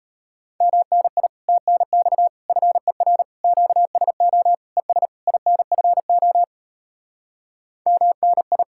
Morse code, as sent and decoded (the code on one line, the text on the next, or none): -- -. .. / - -. -..- / ..-. .-. / --.- ... --- / . ... / .. -. ..-. --- / -- -. ..
MNI TNX FER QSO ES INFO MNI